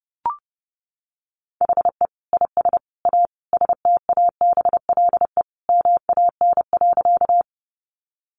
Morse code, heard as a keyed tone, about 28 wpm, 700 Hz. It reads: HE IS A STABLE MAN.
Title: Sentences from Top 1500 Words 30wpm